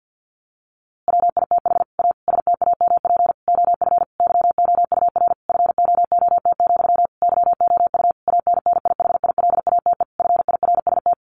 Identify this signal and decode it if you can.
Morse code, heard as a keyed tone, about 50 wpm, 700 Hz. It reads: WITH A STAMP OF YOUR FOOT, YOU ANNIHILATE FIRST
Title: A Sound Of Thunder 50wpm